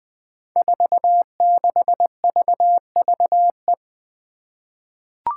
Morse code, heard as a keyed tone, about 20 wpm, 700 Hz. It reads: 46VVE
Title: Five Challenging Letter-Number Combinations 20wpm